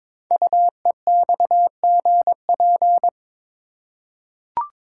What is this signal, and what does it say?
Morse code, heard as a keyed tone, about 22 wpm, 690 Hz. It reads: UEXGP
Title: Code Groups - 5 Letters - 22wpm